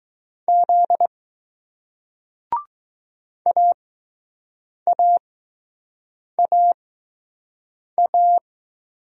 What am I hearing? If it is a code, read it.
Morse code, heard as a keyed tone, about 20 wpm, spaced as for 6 wpm, 700 Hz. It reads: Z AAAA